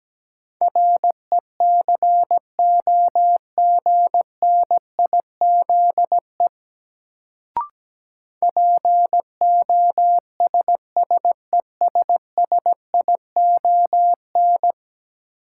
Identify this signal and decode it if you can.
Morse code, heard as a keyed tone, about 17 wpm, 700 Hz. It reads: RECOGNIZE POSSESSION
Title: Sentences from Top 1400 Words - New Word Drill Alt - 17wpm